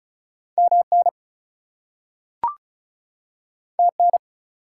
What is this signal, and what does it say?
Morse code, heard as a keyed tone, about 36 wpm, 700 Hz. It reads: MN TN
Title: US States and Canadian Provinces 35wpm